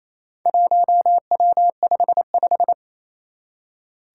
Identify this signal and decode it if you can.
Morse code, heard as a keyed tone, about 27 wpm, 700 Hz. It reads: 1W55
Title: Four Challenging Letter-Number Combinations 28wpm